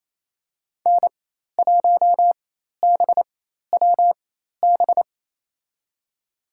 Morse code, heard as a keyed tone, about 28 wpm, spaced as for 13 wpm, 700 Hz. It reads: N1BWB
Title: USA Call Signs - 4x Character Spacing - 28wpm